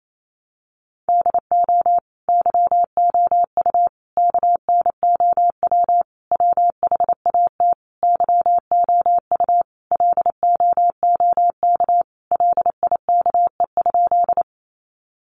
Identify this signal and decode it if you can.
Morse code, heard as a keyed tone, about 28 wpm, 700 Hz. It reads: DO YOU KNOW WHAT YOU LOOK LIKE?
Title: Sentences from Top 100 Words 28wpm